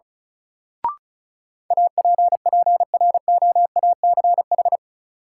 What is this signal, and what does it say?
Morse code, heard as a keyed tone, about 36 wpm, 700 Hz. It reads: APPROACH
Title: Sentences from Top 1000 Words - Review Part 6 - New Word Drill - 35wpm